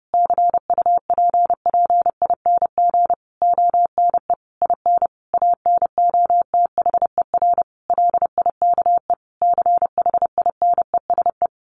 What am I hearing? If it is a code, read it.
Morse code, heard as a keyed tone, about 30 wpm, 700 Hz. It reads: CUPPING ONE IN ANOTHER LIKE CHINESE
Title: A Sound Of Thunder 30wpm